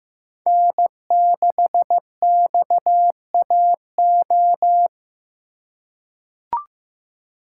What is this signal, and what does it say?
Morse code, heard as a keyed tone, about 15 wpm, 700 Hz. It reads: N6XAO
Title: USA Call Signs - 5 Characters - 2x Character Spacing - 15wpm